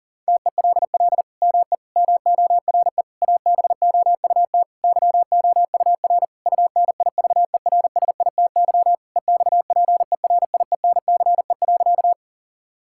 Morse code, heard as a keyed tone, about 40 wpm, 700 Hz. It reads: TELL ME MORE ABOUT YOUR UNIVERSITY EXPERIENCE.
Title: Sentences from Top 1000 Words - Review Part 5 - 40wpm